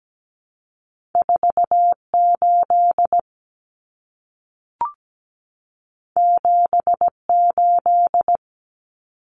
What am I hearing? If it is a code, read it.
Morse code, heard as a keyed tone, about 17 wpm, 700 Hz. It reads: 4 8 7 8
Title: Common Cut Numbers - 2-Digit 17wpm